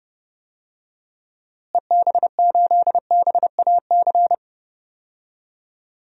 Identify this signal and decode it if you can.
Morse code, heard as a keyed tone, about 29 wpm, 700 Hz. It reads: EB8BAC
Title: DXCC-100 Call Signs - 6 Characters - Repeated 2x - 30wpm